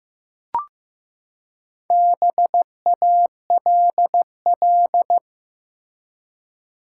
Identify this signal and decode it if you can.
Morse code, heard as a keyed tone, about 15 wpm, 700 Hz. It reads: BALL
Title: Four Letter Words - 15wpm